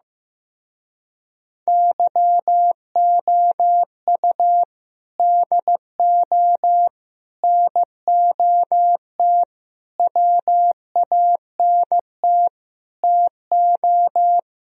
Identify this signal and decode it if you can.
Morse code, heard as a keyed tone, about 15 wpm, 700 Hz. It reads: YOU DO NOT WANT TO
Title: Speed-Racing - Sets of 5 Words 15wpm